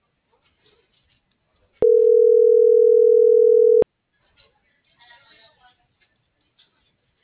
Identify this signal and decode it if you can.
Ringback tone (North America), 440 + 480 Hz.
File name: CENTRAL BLN / out ximena.diaz - .6578